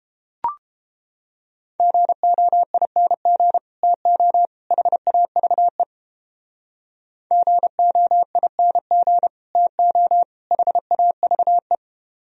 Morse code, heard as a keyed tone, about 29 wpm, 700 Hz. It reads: GOING TO HAVE GOING TO HAVE